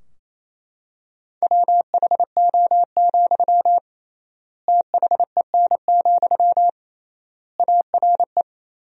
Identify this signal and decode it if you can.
Morse code, heard as a keyed tone, about 28 wpm, 700 Hz. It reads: WHO, THEN, ARE
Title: 3x Word-Spacing - Sentences from Top 100 Words - 28wpm